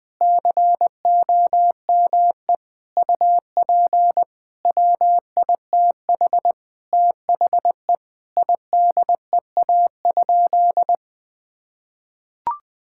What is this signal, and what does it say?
Morse code, heard as a keyed tone, about 20 wpm, 700 Hz. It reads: COME UP WITH THE IDEA?
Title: Sentences with 7-Word Phrases - Spoken Beginning - 20wpm